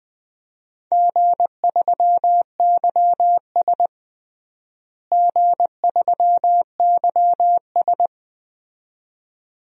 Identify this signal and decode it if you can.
Morse code, heard as a keyed tone, about 20 wpm, 700 Hz. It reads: G3YS G3YS
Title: DXCC-100 Call Signs - 4 Characters - Repeated 2x - 20wpm